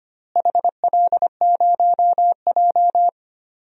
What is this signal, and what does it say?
Morse code, heard as a keyed tone, about 25 wpm, 700 Hz. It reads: HL0J
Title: DXCC-100 Call Signs - 4 Characters - Repeated 3x - 25wpm